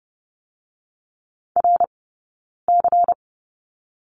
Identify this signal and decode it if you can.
Morse code, heard as a keyed tone, about 30 wpm, 700 Hz. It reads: R C
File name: Lesson 17d - Letter u - Two Characters 30wpm